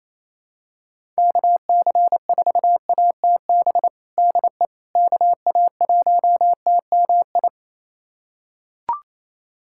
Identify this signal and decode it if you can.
Morse code, heard as a keyed tone, about 28 wpm, 700 Hz. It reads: KC4ATB DE KA1TMI
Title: USA Callsign DE Callsign - 6 Characters - 28wpm